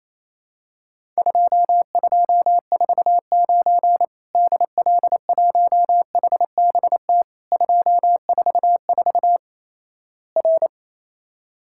Morse code, heard as a keyed tone, about 28 wpm, 700 Hz. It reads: 2249 DL1HBT 244 R
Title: Worked All Europe DX Contest - QTC Exchange - 28wpm